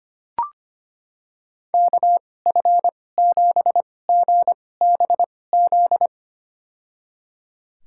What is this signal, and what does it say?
Morse code, heard as a keyed tone, about 25 wpm, spaced as for 18 wpm, 700 Hz. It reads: KF7GBZ
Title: USA Call Signs - 6 Characters - 2x Character Spacing - 25wpm